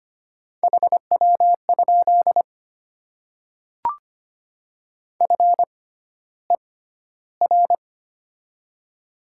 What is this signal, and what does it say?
Morse code, heard as a keyed tone, about 25 wpm, spaced as for 13 wpm, 690 Hz. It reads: HW? FER
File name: Top 20 QSO Elements - 6x Character Spacing - 25wpm